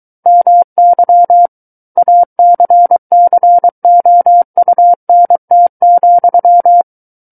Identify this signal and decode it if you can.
Morse code, heard as a keyed tone, about 23 wpm, 700 Hz. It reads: MY ACCOUNT,
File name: Book One / 14-Chapter 14 - 23 WPM